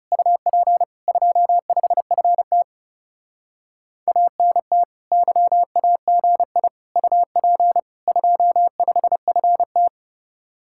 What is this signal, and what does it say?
Morse code, heard as a keyed tone, about 36 wpm, 700 Hz. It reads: UP 25FT ANT YAGI UP 25FT
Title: Speed-Racing Common QSO Phrases Encore - 30wpm